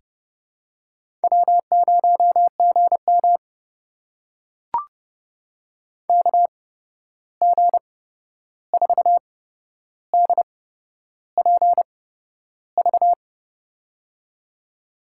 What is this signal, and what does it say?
Morse code, heard as a keyed tone, about 30 wpm, spaced as for 11 wpm, 700 Hz. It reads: W0GM KG4DPV
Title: USA Call Signs - 8x Character Spacing - 30wpm